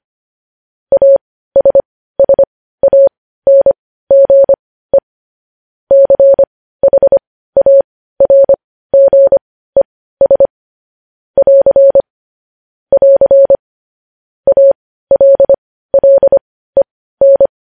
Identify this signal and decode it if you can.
Morse code, heard as a keyed tone, about 25 wpm, spaced as for 15 wpm, 550 Hz. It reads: ASSANGE CHARGES <AR> <AR> ALLEN
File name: podcast / News headlines-1x-25x15